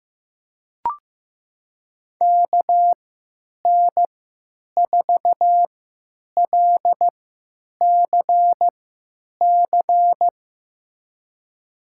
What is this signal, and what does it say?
Morse code, heard as a keyed tone, about 15 wpm, spaced as for 9 wpm, 700 Hz. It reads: KN4LCC
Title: USA Call Signs - 6 Characters - 3x Character Spacing - 15wpm